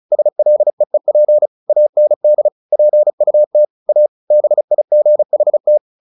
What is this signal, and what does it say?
Morse code, heard as a keyed tone, about 35 wpm, 600 Hz. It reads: SLEEP AND PUT A BIGHT
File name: the old man and the sea / the old man and the sea 1 35wpm